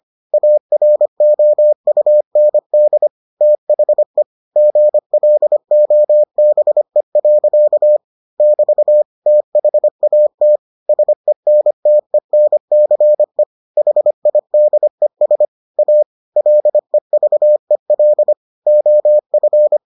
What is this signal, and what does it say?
Morse code, heard as a keyed tone, about 25 wpm, 600 Hz. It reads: AROUND THE GLOBE. <BT> THAT SENTENCE HIDES A LEVEL OF